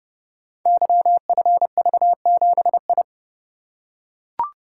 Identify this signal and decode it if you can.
Morse code, heard as a keyed tone, about 29 wpm, 700 Hz. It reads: YFV7I
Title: Code Groups - 5 Letter:Numbers - 30wpm